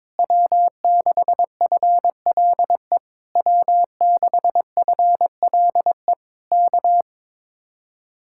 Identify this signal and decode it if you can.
Morse code, heard as a keyed tone, about 22 wpm, 700 Hz. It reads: W6FLE W6FLE K
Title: Speed-Racing Common QSO Phrases - 22wpm